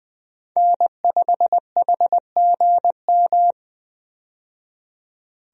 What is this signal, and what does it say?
Morse code, heard as a keyed tone, about 20 wpm, 700 Hz. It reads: N5HGM
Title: USA Call Signs - 5 Characters - 20wpm